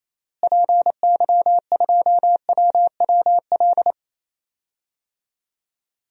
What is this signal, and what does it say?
Morse code, heard as a keyed tone, about 28 wpm, 700 Hz. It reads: PY2WWL
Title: DXCC-100 Call Signs - 6 Characters - Repeated 2x - 28wpm